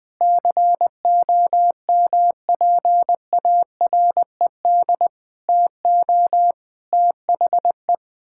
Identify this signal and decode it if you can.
Morse code, heard as a keyed tone, about 20 wpm, 700 Hz. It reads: COMPARED TO THE
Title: Sentences with 7-Word Phrases - 20wpm